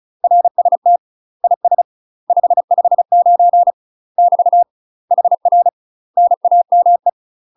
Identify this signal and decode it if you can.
Morse code, heard as a keyed tone, about 34 wpm, 700 Hz. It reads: RST IS 559 <BT> HR NAME
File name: UK MASTER QSOs TRAINING / 35WPM / UK MASTER QSOs 0001